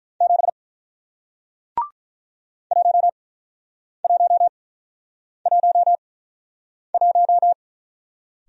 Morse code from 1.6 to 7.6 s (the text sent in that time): 1111